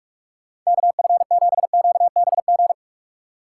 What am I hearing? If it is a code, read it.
Morse code, heard as a keyed tone, about 46 wpm, 700 Hz. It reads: KF7QBG
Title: USA Call Signs - 6 Characters - 4x Character Spacing - 45wpm